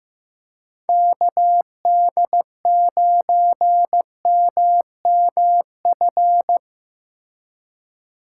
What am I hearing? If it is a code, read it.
Morse code, heard as a keyed tone, about 18 wpm, 700 Hz. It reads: KD9MMF
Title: Speed Racing - USA 2x3 Call Signs 15wpm